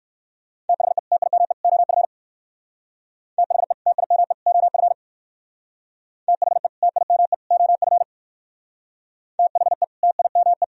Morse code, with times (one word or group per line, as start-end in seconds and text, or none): 0.7-1.0 THE
1.1-1.5 TIME
1.6-2.0 OF
3.4-3.7 THE
3.9-4.3 TIME
4.5-4.9 OF
6.3-6.7 THE
6.8-7.3 TIME
7.5-8.0 OF
9.4-9.8 THE
10.0-10.6 TIME